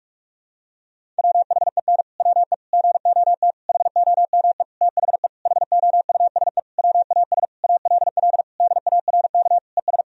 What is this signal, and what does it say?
Morse code, heard as a keyed tone, about 46 wpm, 700 Hz. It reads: WHEN WE GOT HOME THE HOUSE WAS ALL DARK ES
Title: Sentences from Top 500 Words and Abbreviations 45wpm